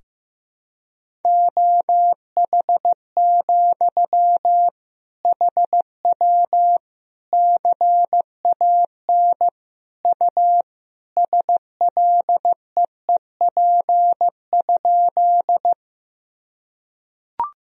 Morse code, heard as a keyed tone, about 15 wpm, 700 Hz. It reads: OH, HW CAN U SLEEP?
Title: Sentences from Top 500 Words and Abbreviations 15wpm